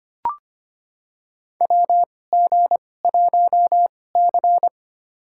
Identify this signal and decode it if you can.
Morse code, heard as a keyed tone, about 25 wpm, spaced as for 18 wpm, 700 Hz. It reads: WG1C
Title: USA Call Signs - 2x Character Spacing - 25wpm